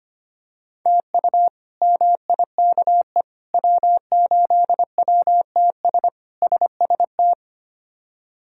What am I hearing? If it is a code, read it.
Morse code, heard as a keyed tone, about 25 wpm, 700 Hz. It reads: TU MIKE W8WTS SST
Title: K1USN SST QSO Practice - 25wpm